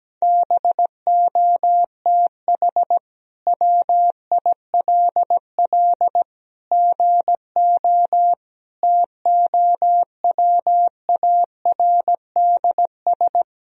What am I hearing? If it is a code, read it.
Morse code, heard as a keyed tone, about 17 wpm, 700 Hz. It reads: BOTH WILL GO TOWARDS
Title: Sentences from Top 1000 Words - Review Part 2 - 17wpm